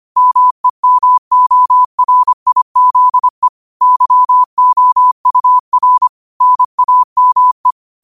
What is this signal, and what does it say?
Morse code, heard as a keyed tone, about 25 wpm, 1000 Hz. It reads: MEMORIZE YOUR NAME
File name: qotdfiles / 25wpm